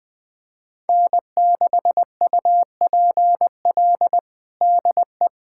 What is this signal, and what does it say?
Morse code, heard as a keyed tone, about 20 wpm, 700 Hz. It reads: N6UPL DE
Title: USA Callsign DE Callsign - 5 Characters - 20wpm